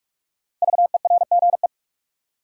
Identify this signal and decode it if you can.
Morse code, heard as a keyed tone, about 46 wpm, 700 Hz. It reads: VERGE
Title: Sets of 4 Words - New Word Drill - 45wpm